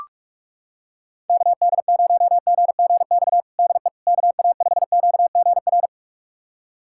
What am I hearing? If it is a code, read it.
Morse code, heard as a keyed tone, about 46 wpm, 700 Hz. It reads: KD0GGX DE KA5QGR
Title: USA Callsign DE Callsign 45wpm